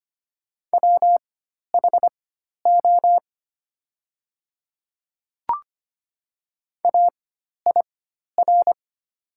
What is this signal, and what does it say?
Morse code, heard as a keyed tone, about 25 wpm, spaced as for 12 wpm, 700 Hz. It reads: WHO AIR
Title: 4x Character Spacing - Top 200 Words - 25wpm